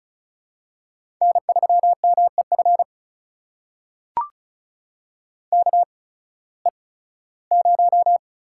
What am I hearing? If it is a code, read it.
Morse code, heard as a keyed tone, about 36 wpm, spaced as for 19 wpm, 700 Hz. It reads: N3MEF KE0
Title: USA Call Signs - 8x Character Spacing - 35wpm